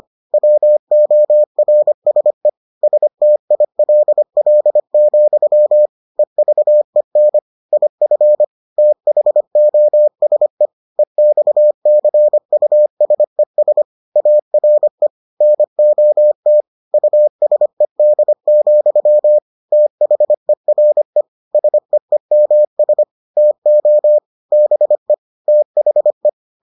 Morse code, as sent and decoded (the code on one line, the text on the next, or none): .-- --- .-. ... / ... - .. .-.. .-.. --..-- / . ...- . -. / .. ..-. / - .... --- ... / . -..- -.-. ..- ... ... / .- .-. / -. --- - / ..- ... -.. --..-- / - .... .-. / ... . -- ... / - --- / -... / - .... .
WORSE STILL, EVEN IF THOSE EXCUSES ARE NOT USED, THERE SEEMS TO BE THE